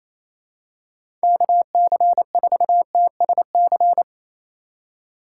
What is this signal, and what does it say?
Morse code, heard as a keyed tone, about 28 wpm, 700 Hz. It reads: KC4TSC
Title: USA Call Signs - 6 Characters - 3x Character Spacing - 28wpm